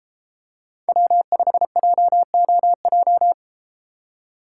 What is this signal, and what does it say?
Morse code, heard as a keyed tone, about 33 wpm, 700 Hz. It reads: W5JOJ